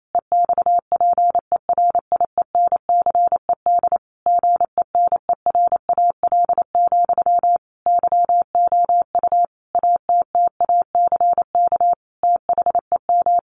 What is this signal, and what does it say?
Morse code, heard as a keyed tone, about 28 wpm, 700 Hz. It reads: EXPERIENCED GENERAL, YOU ATTACK THEM
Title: Sentences from Top 1500 Words 28wpm